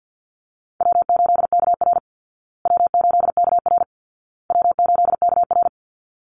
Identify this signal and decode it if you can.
Morse code, heard as a keyed tone, about 50 wpm, 700 Hz. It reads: W8KR W8KR W8KR
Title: Easy Call Signs Repeated 3x - 50wpm